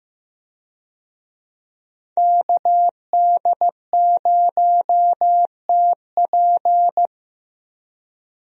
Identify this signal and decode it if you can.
Morse code, heard as a keyed tone, about 15 wpm, 700 Hz. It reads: KD0TP
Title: USA Call Signs - 5 Characters - Repeated 3x - 15wpm